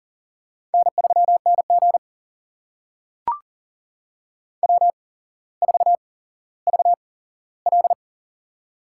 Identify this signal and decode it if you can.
Morse code, heard as a keyed tone, about 42 wpm, spaced as for 17 wpm, 700 Hz. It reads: N3NG W4VL
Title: USA Call Signs - 4 Characters - 8x Character Spacing - 40wpm